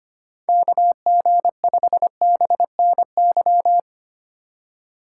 Morse code, heard as a keyed tone, about 30 wpm, 700 Hz. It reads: KG5BNY
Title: Speed Racing - USA 2x3 Call Signs 25wpm